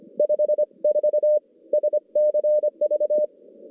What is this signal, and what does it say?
Morse code, heard as a keyed tone, about 26 wpm, 600 Hz. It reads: 54 SCV
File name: contests / CW-SS-k5tr- -2100